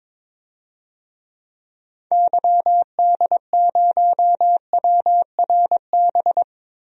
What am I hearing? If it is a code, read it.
Morse code, heard as a keyed tone, about 22 wpm, 700 Hz. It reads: YD0WRB
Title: DXCC-100 Call Signs - 6 Characters - Repeated 3x - 22wpm